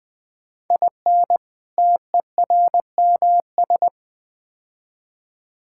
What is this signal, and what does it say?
Morse code, heard as a keyed tone, about 20 wpm, 700 Hz. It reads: IN TERMS
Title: Build A Phrase - 4-Word Phrases - 20wpm